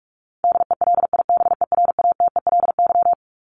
Morse code, heard as a keyed tone, about 44 wpm, 700 Hz. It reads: DELIBERATELY
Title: Top 1751-2000 Words 45wpm